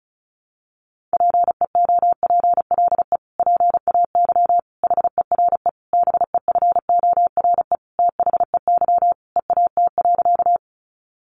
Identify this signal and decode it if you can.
Morse code, heard as a keyed tone, about 34 wpm, 700 Hz. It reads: PEOPLE PAY HERE BEFORE THEY EAT.
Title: Sentences from Top 700 Words 35wpm